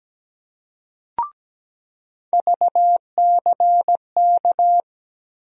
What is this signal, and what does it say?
Morse code, heard as a keyed tone, about 17 wpm, 700 Hz. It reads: VCK
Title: USA Call Sign Postfixes - 3 Letters 17wpm